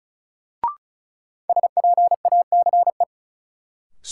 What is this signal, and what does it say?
Morse code, heard as a keyed tone, about 36 wpm, 700 Hz. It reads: SPACE
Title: Sentences from Top 1500 Words - New Word Drill Alt - 35wpm